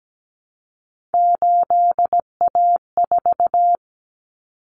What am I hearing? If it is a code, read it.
Morse code, heard as a keyed tone, about 17 wpm, 700 Hz. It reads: 8A4
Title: Common Cut Numbers - 3-Digit 17wpm